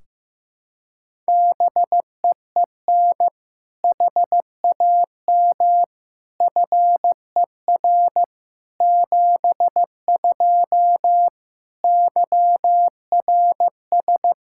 Morse code, heard as a keyed tone, about 15 wpm, 700 Hz. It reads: BEEN HAM FER 72 YRS
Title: K1USN SST QSO Practice 15wpm